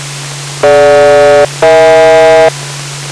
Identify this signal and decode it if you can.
Aircraft SELCAL, heard as two pairs of tones. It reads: DG-EH